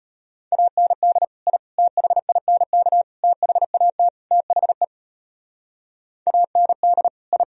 Morse code, heard as a keyed tone, about 33 wpm, 700 Hz. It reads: AND I THINK THAT THE AND I